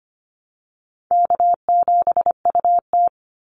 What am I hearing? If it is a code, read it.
Morse code, heard as a keyed tone, about 25 wpm, 700 Hz. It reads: K7UT
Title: Easy Call Signs Repeated 2x - 25wpm